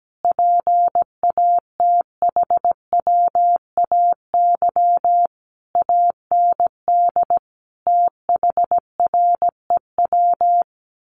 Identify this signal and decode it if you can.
Morse code, heard as a keyed tone, about 17 wpm, 700 Hz. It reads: PATHWAY AND THREW
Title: A Sound Of Thunder 17wpm